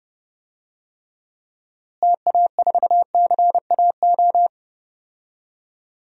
Morse code, heard as a keyed tone, about 30 wpm, 700 Hz. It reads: TA4CAO
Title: DXCC-100 Call Signs - 6 Characters - Repeated 2x - 30wpm